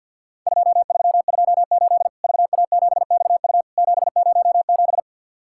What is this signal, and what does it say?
Morse code, heard as a keyed tone, about 50 wpm, 700 Hz. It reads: 2328 VA7XU 707
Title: Worked All Europe DX Contest - QTC Exchange with Repeats - 50wpm